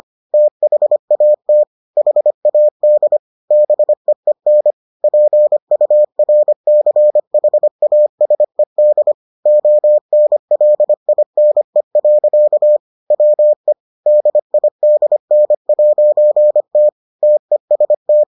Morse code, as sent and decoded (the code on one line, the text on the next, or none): - .... .- - / .... .- -.. / -... . -. / .--. ..- .-. -.-. .... .- ... -.. / --- -. .-.. .. -. .-.-.- / .-- . / -.. .. -.. -. .----. - / - . ... -
THAT HAD BEEN PURCHASED ONLINE. WE DIDN'T TEST